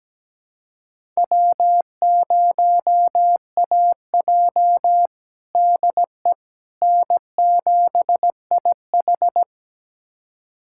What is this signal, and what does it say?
Morse code, heard as a keyed tone, about 17 wpm, 700 Hz. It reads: W0AJ DE N7IH
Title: USA Callsign DE Callsign - 4 Characters - 17wpm